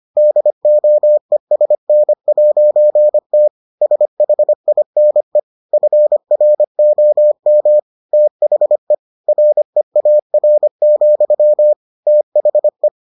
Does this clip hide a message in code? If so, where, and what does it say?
Yes: Morse code from 0.2 to 12.9 s, heard as a keyed tone, about 25 wpm, 600 Hz. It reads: DOESN'T SHINE FROM THE REAR, THE